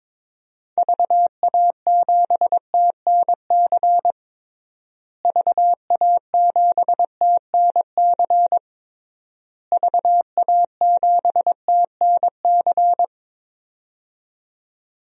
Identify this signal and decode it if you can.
Morse code, heard as a keyed tone, about 21 wpm, 700 Hz. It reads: VA7TNC VA7TNC VA7TNC
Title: DXCC-100 Call Signs - 6 Characters - Repeated 3x - 22wpm